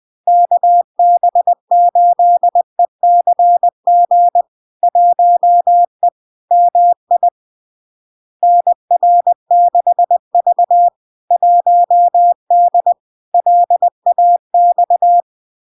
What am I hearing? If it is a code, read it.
Morse code, heard as a keyed tone, about 20 wpm, 700 Hz. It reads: KB8ECG 1E MI NR6V 1D LAX
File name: LARK-FD-2022-Log-in-CW-20wpm-700-Hz